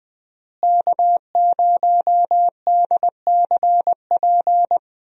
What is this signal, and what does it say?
Morse code, heard as a keyed tone, about 20 wpm, 700 Hz. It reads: K0DCP
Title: USA Call Signs - 5 Characters - Repeated 2x - 20wpm